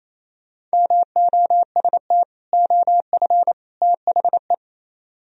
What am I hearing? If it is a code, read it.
Morse code, heard as a keyed tone, about 34 wpm, 700 Hz. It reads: MOST OF THE
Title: Speed-Racing - Sets of 3 Words 28wpm